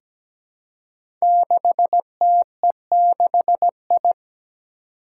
Morse code, heard as a keyed tone, about 17 wpm, 700 Hz. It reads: 6TE6I
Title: Five Challenging Letter-Number Combinations 17wpm